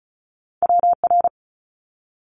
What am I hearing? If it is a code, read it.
Morse code, heard as a keyed tone, about 34 wpm, 700 Hz. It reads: WR
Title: Multi-letter Phonemes Quick Fire 35wpm